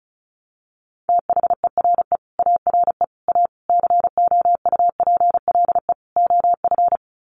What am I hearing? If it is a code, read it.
Morse code, heard as a keyed tone, about 34 wpm, 700 Hz. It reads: THERE ARE A COUPLE OF
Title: Sets of 5 Words Encore 35wpm